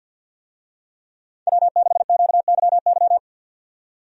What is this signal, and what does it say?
Morse code, heard as a keyed tone, about 48 wpm, 700 Hz. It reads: W6QYY
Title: USA Call Signs - 5 Characters - 50wpm